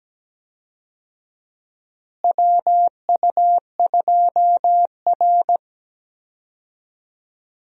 Morse code, heard as a keyed tone, about 17 wpm, 700 Hz. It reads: WU2R
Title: USA Call Signs - 4 Characters - Repeated 2x - 17wpm